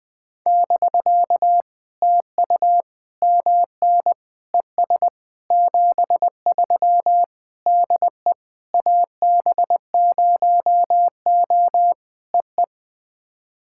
Morse code, heard as a keyed tone, about 20 wpm, 700 Hz. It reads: <BK> TU MN ES 73 DE AB0O EE
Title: POTA QSO Practice - 20wpm